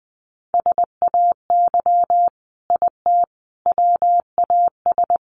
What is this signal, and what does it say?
Morse code, heard as a keyed tone, about 19 wpm, 700 Hz. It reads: SAY IT WAS
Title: Sentences from Top 800 Words 20wpm